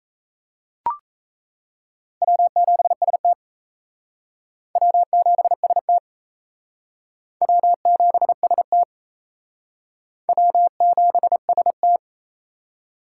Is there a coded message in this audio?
Yes: Morse code from 2.2 to 12.0 s, heard as a keyed tone, about 37 wpm, 700 Hz. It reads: W7ST W7ST W7ST W7ST